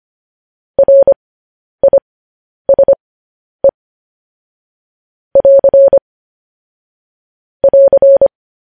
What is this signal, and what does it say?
Morse code, heard as a keyed tone, about 25 wpm, spaced as for 10 wpm, 550 Hz. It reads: RISE <AR> <AR>